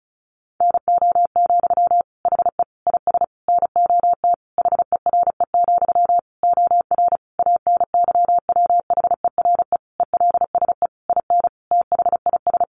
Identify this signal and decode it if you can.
Morse code, heard as a keyed tone, about 34 wpm, 700 Hz. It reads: NO, HE IS NOT HERE, OR ANYWHERE ELSE IN THIS
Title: Sentences from Top 1400 Words 35wpm